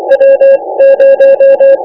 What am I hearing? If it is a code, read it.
Morse code, heard as a keyed tone, about 24 wpm, 530 Hz. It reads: W 0
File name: iw0ffk / 2009 07 03 JO6EDD IW0FFK rec by jo6edd